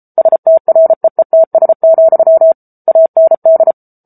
Morse code, heard as a keyed tone, about 34 wpm, 650 Hz. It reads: STREETS, AND
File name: Book Two / 07-Chapter 7 - 33 WPM